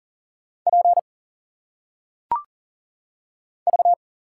Morse code, heard as a keyed tone, about 40 wpm, 700 Hz. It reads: P V
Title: Single Letters - Phonetic Pronunciation 40wpm